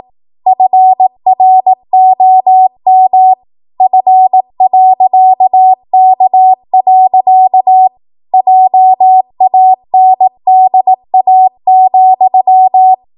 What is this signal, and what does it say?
Morse code, heard as a keyed tone, about 18 wpm, 750 Hz. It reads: FROM F.K. JANDA,
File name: Archive / 18 WPM / 18WPM